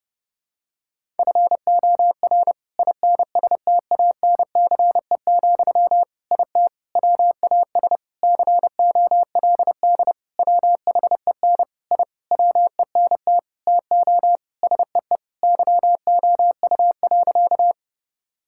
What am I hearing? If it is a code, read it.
Morse code, heard as a keyed tone, about 32 wpm, 700 Hz. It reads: FOR INSTANCE, IT WAS COLD WHEN I WENT TO SEE YOU.
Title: Sentences from Top 1000 Words - Review Part 5 - 30wpm